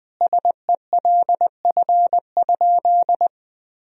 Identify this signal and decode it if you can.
Morse code, heard as a keyed tone, about 20 wpm, 700 Hz. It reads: SELF?
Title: Sentences from Top 1000 Words - Review Part 1 - 20wpm